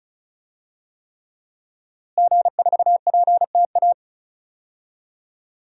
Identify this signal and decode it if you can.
Morse code, heard as a keyed tone, about 36 wpm, 700 Hz. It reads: G4PTA
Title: DXCC-100 Call Signs - 5 Characters - Repeated 3x - 35wpm